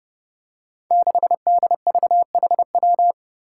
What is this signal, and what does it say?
Morse code, heard as a keyed tone, about 30 wpm, 700 Hz. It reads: 6DVHW
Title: Five Challenging Letter-Number Combinations 30wpm